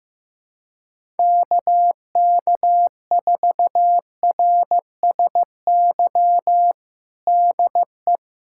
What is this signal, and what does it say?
Morse code, heard as a keyed tone, about 15 wpm, 700 Hz. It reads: KK4RSY DE
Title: USA Callsign DE Callsign - 6 Characters - 15wpm